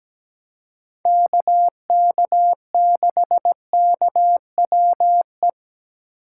Morse code, heard as a keyed tone, about 20 wpm, 700 Hz. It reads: KK6KWE
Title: Speed Racing - USA 2x3 Call Signs 17wpm